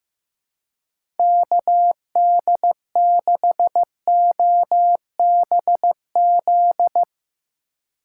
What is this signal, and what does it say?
Morse code, heard as a keyed tone, about 16 wpm, 700 Hz. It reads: KD6OBZ
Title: USA Call Signs - 6 Characters - 4x Character Spacing - 15wpm